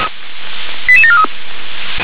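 Five-tone selective call (ZVEI), heard as quick runs of five tones, tones of about 70 ms. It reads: 88743
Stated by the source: ZVEI five-tone selective call received by SDR.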